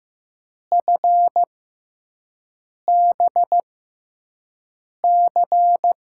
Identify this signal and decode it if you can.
Morse code, heard as a keyed tone, about 15 wpm, spaced as for 5 wpm, 700 Hz. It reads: FBC
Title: USA Call Signs - 5 Characters - 6x Character Spacing - 15wpm